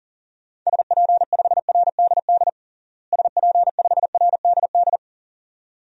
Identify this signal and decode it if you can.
Morse code, heard as a keyed tone, about 40 wpm, 700 Hz. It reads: SP5RDD SP5RDD
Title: DXCC-100 Call Signs - 6 Characters - Repeated 2x - 40wpm